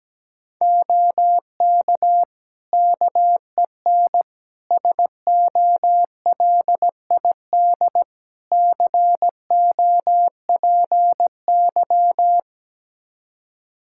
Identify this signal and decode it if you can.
Morse code, heard as a keyed tone, about 17 wpm, 700 Hz. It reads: OK KEN SOLID COPY
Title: Practice QSO - Exchange 3 - More Info - 17wpm